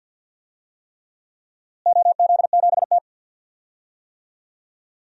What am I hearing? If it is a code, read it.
Morse code, heard as a keyed tone, about 51 wpm, 700 Hz. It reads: OZ7T
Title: DXCC-100 Call Signs - 4 Characters - Repeated 3x - 50wpm